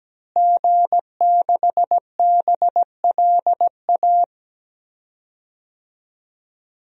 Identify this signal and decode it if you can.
Morse code, heard as a keyed tone, about 17 wpm, 700 Hz. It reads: G6BLA
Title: DXCC-100 Call Signs - 5 Characters - Repeated 2x - 17wpm